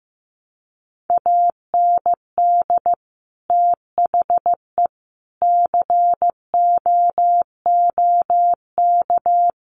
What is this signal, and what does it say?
Morse code, heard as a keyed tone, about 15 wpm, 700 Hz. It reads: AND THE COOK
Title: Sentences from Top 1300 Words 15wpm